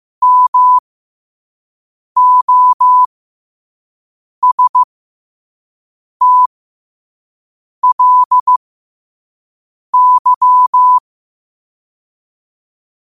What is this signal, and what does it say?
Morse code, heard as a keyed tone, about 15 wpm, spaced as for 5 wpm, 1000 Hz. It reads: MOSTLY
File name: qotdfiles / 5wpm